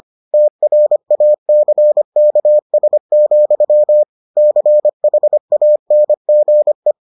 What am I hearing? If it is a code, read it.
Morse code, heard as a keyed tone, about 25 wpm, 600 Hz. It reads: TRACKS, CHANGE